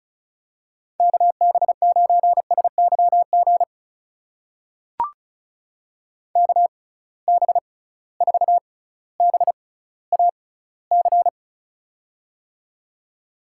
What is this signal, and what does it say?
Morse code, heard as a keyed tone, about 36 wpm, spaced as for 18 wpm, 700 Hz. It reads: KB9SYG KB4BAC
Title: USA Call Signs - 6 Characters - 6x Character Spacing - 35wpm